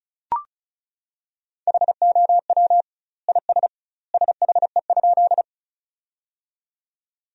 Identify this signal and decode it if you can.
Morse code, heard as a keyed tone, about 36 wpm, 700 Hz. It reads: HOW IS SHE?
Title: 2x Word-Spacing - Sentences from Top 100 Words - 35wpm